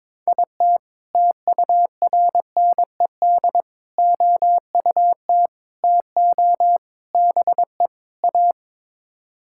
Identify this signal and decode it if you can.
Morse code, heard as a keyed tone, about 22 wpm, 700 Hz. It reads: IT TURNED OUT TO BE A
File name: practice / 22wpm / Sets of 6 Words - 22wpm